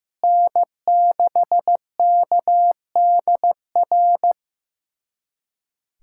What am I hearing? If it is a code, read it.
Morse code, heard as a keyed tone, about 15 wpm, 700 Hz. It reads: N6KDR
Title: USA Callsign DE Callsign - 5 Characters - 15wpm